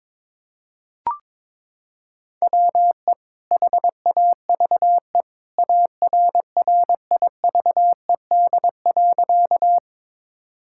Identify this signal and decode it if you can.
Morse code, heard as a keyed tone, about 22 wpm, 700 Hz. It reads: WE HAVE ARRIVED.
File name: practice / 22wpm / Sentences with 2-Word Phrases - 22wpm